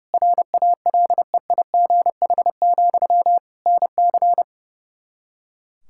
Morse code, heard as a keyed tone, about 30 wpm, 700 Hz. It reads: RALEIGH, NC
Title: Practice QSO - Exchange 1 - Beginning - 30wpm